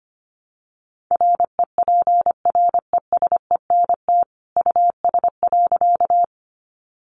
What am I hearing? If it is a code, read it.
Morse code, heard as a keyed tone, about 25 wpm, 700 Hz. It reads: REPRESENT US.
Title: Sentences from Top 1200 Words 25wpm